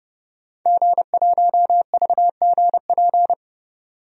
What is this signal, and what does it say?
Morse code, heard as a keyed tone, about 30 wpm, 700 Hz. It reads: G1VGP
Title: DXCC-100 Call Signs - 5 Characters - 30wpm